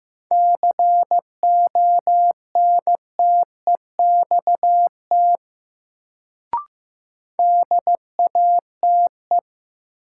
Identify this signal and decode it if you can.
Morse code, heard as a keyed tone, about 15 wpm, 700 Hz. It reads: CONTEXT DATE
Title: Sentences from Top 1000 Words - Review Part 7 - New Word Drill - 15wpm